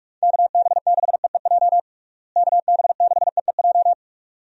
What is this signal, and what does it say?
Morse code, heard as a keyed tone, about 46 wpm, 700 Hz. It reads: KB6EEJ KB6EEJ
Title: USA Call Signs - 6 Characters - Repeated 2x - 45wpm